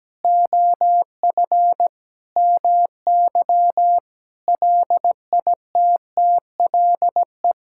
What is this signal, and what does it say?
Morse code, heard as a keyed tone, about 17 wpm, 700 Hz. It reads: OF MY LITTLE
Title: Sentences from Top 1600 Words 17wpm